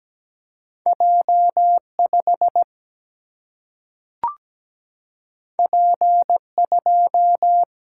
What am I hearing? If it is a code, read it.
Morse code, heard as a keyed tone, about 17 wpm, 700 Hz. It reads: J5 P2
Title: Single-Letter Single-Number 17wpm